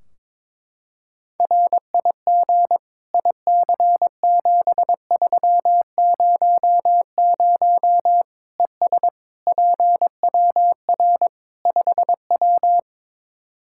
Morse code, heard as a keyed tone, about 22 wpm, 700 Hz. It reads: RIG IC7300 ES PWR 5W
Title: Practice QSO - Exchange 2 - Info - 22wpm